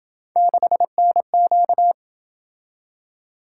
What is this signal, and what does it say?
Morse code, heard as a keyed tone, about 26 wpm, 700 Hz. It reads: 6NQ